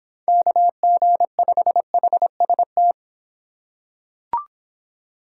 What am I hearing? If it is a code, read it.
Morse code, heard as a keyed tone, about 26 wpm, 700 Hz. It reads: KG5HST